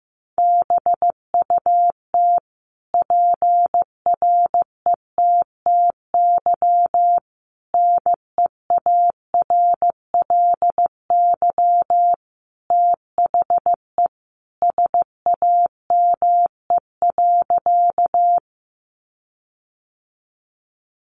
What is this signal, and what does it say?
Morse code, heard as a keyed tone, about 15 wpm, 700 Hz. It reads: BUT PRETTY NEARLY THE SAME.
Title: Sentences from Top 1300 Words 15wpm